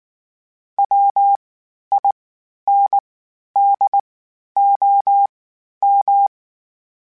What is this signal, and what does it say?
Morse code, heard as a keyed tone, about 19 wpm, spaced as for 11 wpm, 800 Hz. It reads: WINDOM
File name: (spaced) / FromSMs spaced mp3s / Antennas sp@18